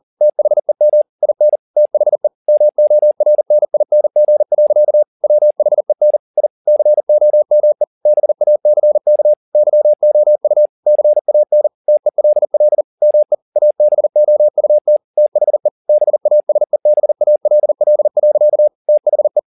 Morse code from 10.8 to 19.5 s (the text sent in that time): CAN TELL ME ABOUT THE BASEBALL. THE